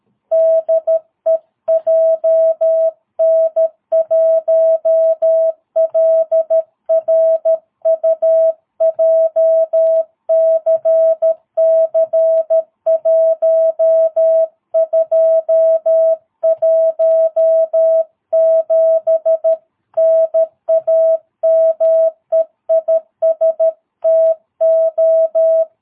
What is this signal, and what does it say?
Morse code, heard as a keyed tone, about 13 wpm, 650 Hz. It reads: DE JN1LRU JCC1217 NAME IS TO
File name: jh8cbh / 2022cw